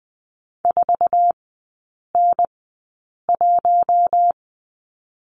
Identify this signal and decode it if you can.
Morse code, heard as a keyed tone, about 20 wpm, spaced as for 8 wpm, 700 Hz. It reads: 4N1
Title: 218 Lesson 35e - Number 0 - Three Characters 20wpm